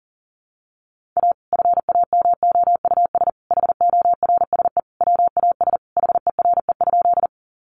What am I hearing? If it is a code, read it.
Morse code, heard as a keyed tone, about 37 wpm, 700 Hz. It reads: A FAMOUS HORSE WAS HERE?
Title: Sentences from Top 1500 Words 40wpm